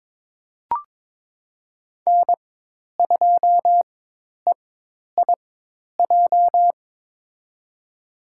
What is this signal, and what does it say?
Morse code, heard as a keyed tone, about 22 wpm, spaced as for 10 wpm, 700 Hz. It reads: N2EIJ
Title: USA Call Signs - 5 Characters - 4x Character Spacing - 22wpm